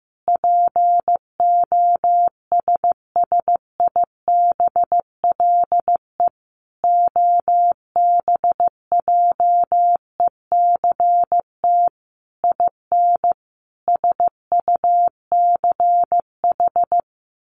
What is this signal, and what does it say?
Morse code, heard as a keyed tone, about 15 wpm, 700 Hz. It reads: POSSIBLE OBJECT IN SUCH
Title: Sentences from Top 1500 Words 15wpm